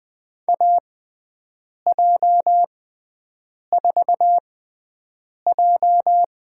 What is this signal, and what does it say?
Morse code, heard as a keyed tone, about 21 wpm, spaced as for 7 wpm, 700 Hz. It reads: AJ4J
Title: USA Call Signs - 4 Characters - 6x Character Spacing - 20wpm